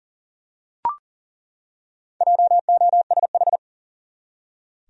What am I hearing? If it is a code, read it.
Morse code, heard as a keyed tone, about 40 wpm, 700 Hz. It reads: JOSH
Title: USA Common Names - Max 4 Letters - 40wpm